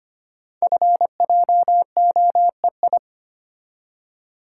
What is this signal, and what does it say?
Morse code, heard as a keyed tone, about 25 wpm, 690 Hz. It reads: FJOEI
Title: Code Groups - 5 Letters - 25wpm